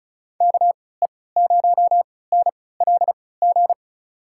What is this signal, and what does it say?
Morse code, heard as a keyed tone, about 36 wpm, spaced as for 20 wpm, 700 Hz. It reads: KE0NLG
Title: USA Call Signs - 6 Characters - 3x Character Spacing - 35wpm